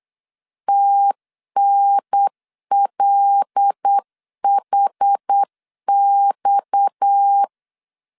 Morse code, heard as a keyed tone, about 8 wpm, 800 Hz. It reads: TNLHX
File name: LEZIONI / 7 GF9Z / 7 GF 8wpm 3108